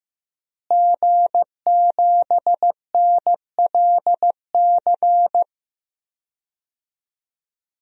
Morse code, heard as a keyed tone, about 15 wpm, 700 Hz. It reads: G7NLC
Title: DXCC-100 Call Signs - 5 Characters - Repeated 2x - 15wpm